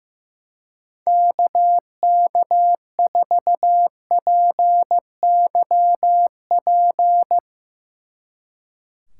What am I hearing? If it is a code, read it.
Morse code, heard as a keyed tone, about 16 wpm, 700 Hz. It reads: KK4PYP
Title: Speed Racing - USA Call Signs 15wpm